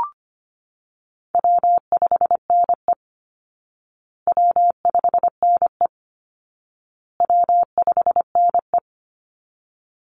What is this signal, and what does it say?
Morse code, heard as a keyed tone, about 25 wpm, 700 Hz. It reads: W5NE W5NE W5NE
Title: Call Signs Repeated 2x - 25wpm